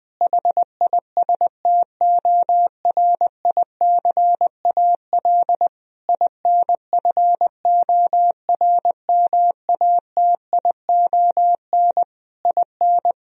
Morse code, heard as a keyed tone, about 20 wpm, 700 Hz. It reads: HISTORICAL INFORMATION IN
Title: Sentences from Top 1000 Words - Review Part 4 - 20wpm